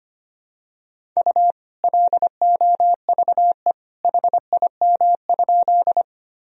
Morse code, heard as a keyed tone, about 25 wpm, 700 Hz. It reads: U LOVE HIM?
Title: Sentences from Top 400 Words and Abbreviations 25wpm